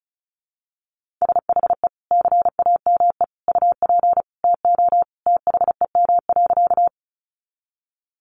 Morse code, heard as a keyed tone, about 34 wpm, 700 Hz. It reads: SHE CAME UP TO THEM.
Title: Sentences from Top 200 Words 35wpm